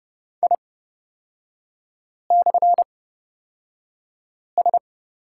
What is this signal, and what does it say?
Morse code, heard as a keyed tone, about 30 wpm, spaced as for 5 wpm, 700 Hz. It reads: I/S
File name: Lesson 38e - Letter q - Three Characters 30wpm